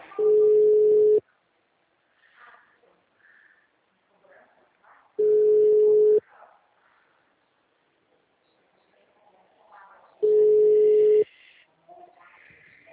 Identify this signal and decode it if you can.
Ringback tone (Europe, 425 Hz).
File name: out- -2010- - - .45545